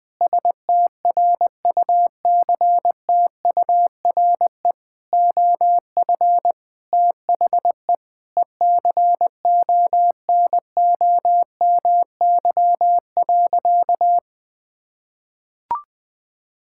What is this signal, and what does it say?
Morse code, heard as a keyed tone, about 20 wpm, 700 Hz. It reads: STRUCTURE OF THE ECONOMY.
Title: Sentences from Top 1000 Words - Review Part 6 - 20wpm